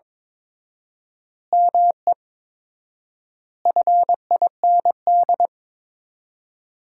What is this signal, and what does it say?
Morse code, heard as a keyed tone, about 21 wpm, 700 Hz. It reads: ME FIND
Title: 4x Word-Spacing - Sentences from Top 200 Words - 22wpm